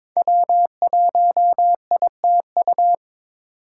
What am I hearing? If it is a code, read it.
Morse code, heard as a keyed tone, about 21 wpm, 700 Hz. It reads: W1ITU
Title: USA Call Signs - 5 Characters - 3x Character Spacing - 22wpm